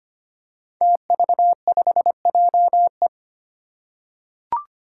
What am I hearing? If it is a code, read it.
Morse code, heard as a keyed tone, about 25 wpm, 700 Hz. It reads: TV5JE
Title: Five Challenging Letter-Number Combinations 25wpm